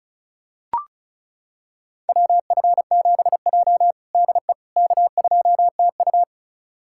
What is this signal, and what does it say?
Morse code, heard as a keyed tone, about 34 wpm, 700 Hz. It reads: WF7J DE K2TU
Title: USA Callsign DE Callsign - 4 Characters - 35wpm